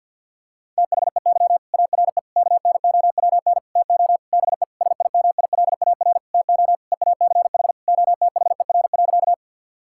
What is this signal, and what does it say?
Morse code, heard as a keyed tone, about 50 wpm, 700 Hz. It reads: THEY ARE KNOWN TO BE SIMILAR TO EACH OTHER.
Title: Sentences from Top 1000 Words - Review Part 5 - 50wpm